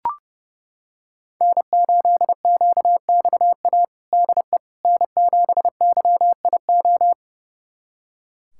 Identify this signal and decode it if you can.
Morse code, heard as a keyed tone, about 31 wpm, 700 Hz. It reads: N8QXA DE N7YIO
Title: USA Callsign DE Callsign - 5 Characters - 30wpm